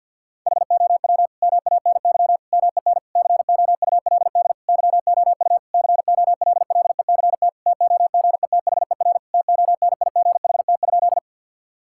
Morse code, heard as a keyed tone, about 52 wpm, 700 Hz. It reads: HOW MANY MEN COULD YOU COLLECT TOGETHER TONIGHT?
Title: Sentences from Top 1600 Words 50wpm